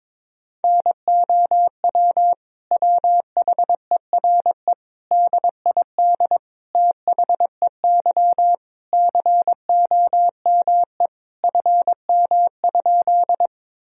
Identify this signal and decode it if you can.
Morse code, heard as a keyed tone, about 22 wpm, 700 Hz. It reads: NOW WHERE DID THEY COME FM?
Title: Sentences from Top 200 Words and Abbreviations 22wpm